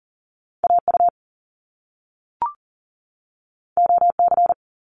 Morse code, heard as a keyed tone, about 40 wpm, 700 Hz. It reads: AU OC
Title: Top Two Letter Combinations in Words 40wpm